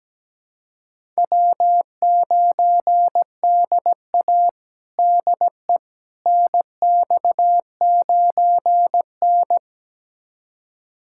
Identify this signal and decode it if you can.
Morse code, heard as a keyed tone, about 17 wpm, 700 Hz. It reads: W9DA DE NX9N
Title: USA Callsign DE Callsign - 4 Characters - 17wpm